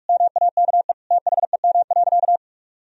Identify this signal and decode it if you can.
Morse code, heard as a keyed tone, about 46 wpm, 700 Hz. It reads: MAKE THEM.
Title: Sentences from Top 1000 Words - Review Part 1 - 45wpm